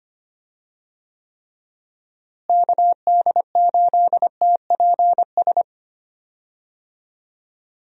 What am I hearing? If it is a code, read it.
Morse code, heard as a keyed tone, about 25 wpm, 700 Hz. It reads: KD8TPS
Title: USA Call Signs - 6 Characters - Repeated 2x - 25wpm